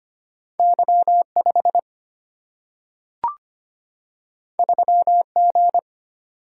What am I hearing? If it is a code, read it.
Morse code, heard as a keyed tone, about 25 wpm, 700 Hz. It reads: Y5 3G
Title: Single-Letter Single-Number 25wpm